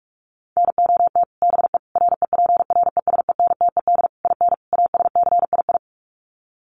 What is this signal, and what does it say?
Morse code, heard as a keyed tone, about 44 wpm, 700 Hz. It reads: NOT BE REPRESENTED IN ASCII
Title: Sets of 5 Words 45wpm